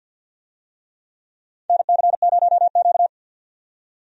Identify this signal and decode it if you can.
Morse code, heard as a keyed tone, about 50 wpm, 700 Hz. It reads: NC0Q
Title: USA Call Signs - 4 Characters - 50wpm